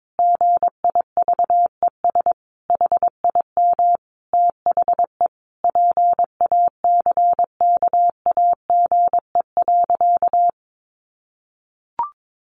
Morse code, heard as a keyed tone, about 22 wpm, 700 Hz. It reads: GIVES HIM THE PACKAGE.
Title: Additional Easy Phrases 22wpm